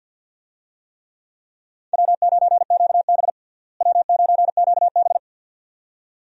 Morse code, heard as a keyed tone, about 52 wpm, 700 Hz. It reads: W9QB W9QB
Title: USA Call Signs - 4 Characters - Repeated 2x - 50wpm